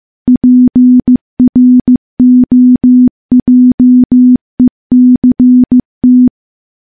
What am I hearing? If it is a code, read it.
Morse code, heard as a keyed tone, about 15 wpm, 250 Hz. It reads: PROJECT
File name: morse / morse-training-15WPM